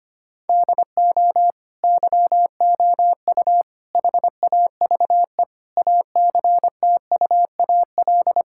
Morse code, heard as a keyed tone, about 25 wpm, 700 Hz. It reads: DO YOU HAVE ACTUAL
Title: Sentences from Top 1000 Words - Review Part 5 - 25wpm